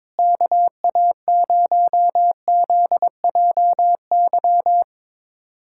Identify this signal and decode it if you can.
Morse code, heard as a keyed tone, about 22 wpm, 700 Hz. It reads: KA0ZJY
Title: USA Callsign DE Callsign 22wpm